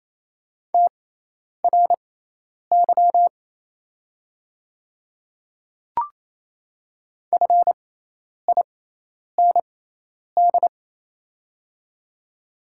Morse code, heard as a keyed tone, about 27 wpm, spaced as for 10 wpm, 700 Hz. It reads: TRY FIND
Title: 6x Character Spacing - Top 300 Words - 28wpm